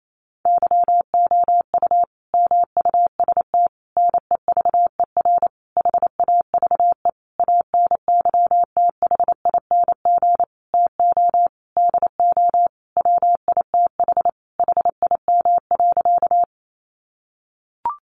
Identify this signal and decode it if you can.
Morse code, heard as a keyed tone, about 28 wpm, 700 Hz. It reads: YOU MUST NEVER HAVE ANYTHING TO DO WITH HIM.
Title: Sentences from Top 1000 Words 28wpm